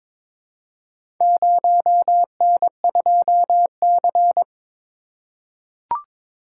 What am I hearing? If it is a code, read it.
Morse code, heard as a keyed tone, about 22 wpm, 700 Hz. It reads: 0N2C
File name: practice / 22wpm / Four Letter-Number Combinations - 22wpm